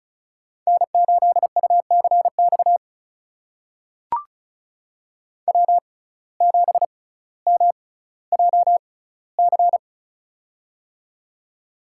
Morse code, heard as a keyed tone, about 34 wpm, spaced as for 18 wpm, 700 Hz. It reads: N8UCX W7MJC
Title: USA Call Signs - 5 Characters - 6x Character Spacing - 35wpm